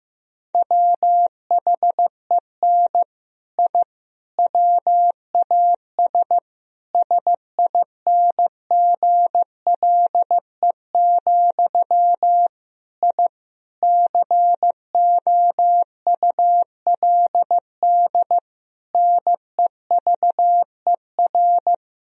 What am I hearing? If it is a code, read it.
Morse code, heard as a keyed tone, about 15 wpm, 700 Hz. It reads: WHEN I WAS SINGLE, I COULD NEVER
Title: Sentences from Top 1000 Words - Review Part 4 - 15wpm